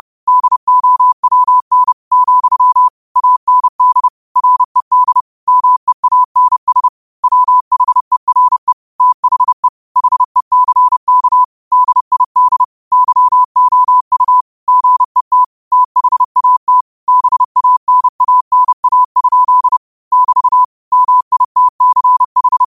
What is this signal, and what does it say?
Morse code, heard as a keyed tone, about 30 wpm, 1000 Hz. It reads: DOWN , AND RED MEANS WHERE THE HECK DID YOU GET THAT BANANA? <BT> MITCH
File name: qotdfiles / 30wpm